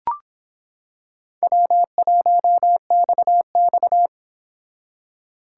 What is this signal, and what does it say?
Morse code, heard as a keyed tone, about 25 wpm, 700 Hz. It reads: W1XX